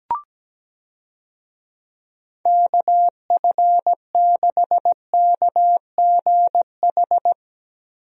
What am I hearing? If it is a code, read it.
Morse code, heard as a keyed tone, about 17 wpm, 700 Hz. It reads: KF6KGH
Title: USA Call Signs - 6 Characters - Repeated 3x - 17wpm